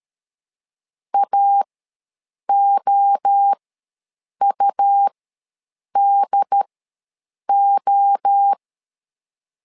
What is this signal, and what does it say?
Morse code, heard as a keyed tone, about 13 wpm, spaced as for 7 wpm, 800 Hz. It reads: AOUDO
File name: LEZIONI / 3 MWRO / 3 MWRO ws 12wpm 1914